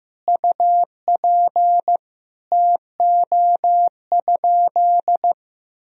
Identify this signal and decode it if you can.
Morse code, heard as a keyed tone, about 15 wpm, 700 Hz. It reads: UP TO?
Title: Sentences from Top 100 Words and Abbreviations 15wpm